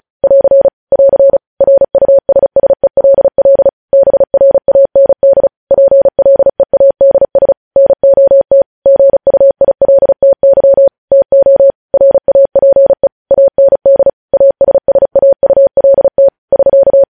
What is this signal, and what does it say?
Morse code, heard as a keyed tone, about 36 wpm, 550 Hz. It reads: <AR> <AR> RUSSELL BRAND PLEADS NOT GUILTY TO RAPE AND ASSAULT <SK>
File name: podcast / News headlines-1x-35x35